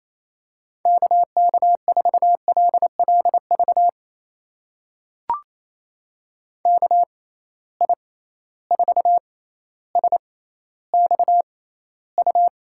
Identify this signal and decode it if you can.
Morse code, heard as a keyed tone, about 28 wpm, spaced as for 14 wpm, 700 Hz. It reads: KK4LLV KI4SXU
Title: USA Call Signs - 6 Characters - 6x Character Spacing - 28wpm